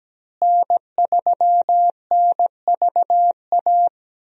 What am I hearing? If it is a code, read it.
Morse code, heard as a keyed tone, about 17 wpm, 700 Hz. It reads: N3NVA
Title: USA Call Signs - 5 Characters - Repeated 3x - 17wpm